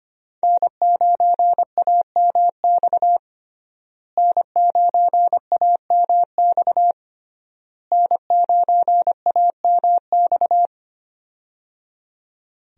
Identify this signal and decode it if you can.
Morse code, heard as a keyed tone, about 25 wpm, 700 Hz. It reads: N9AMX N9AMX N9AMX
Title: USA Call Signs - 5 Characters - Repeated 3x - 25wpm